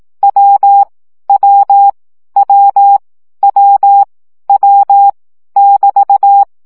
Morse code, heard as a keyed tone, about 18 wpm, 800 Hz. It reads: W W W W W <BT>
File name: CW gyakorlas w 15 baud